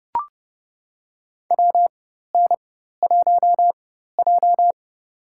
Morse code, heard as a keyed tone, about 30 wpm, spaced as for 14 wpm, 700 Hz. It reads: WN1J
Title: USA Call Signs - 4 Characters - 4x Character Spacing - 30wpm